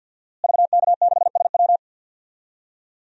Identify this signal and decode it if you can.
Morse code, heard as a keyed tone, about 49 wpm, 700 Hz. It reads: VK6SW
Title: DXCC-100 Call Signs - 5 Characters - Repeated 3x - 50wpm